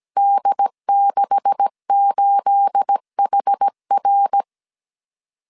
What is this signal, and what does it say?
Morse code, heard as a keyed tone, about 17 wpm, 800 Hz. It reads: D68HR
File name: CW / DET / 1 9 1500C S C 16wpm 2909